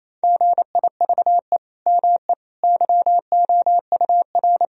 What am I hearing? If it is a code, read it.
Morse code, heard as a keyed tone, about 28 wpm, 700 Hz. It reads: GIVE ME YOUR
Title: Sentences from Top 1000 Words - Review Part 4 - 28wpm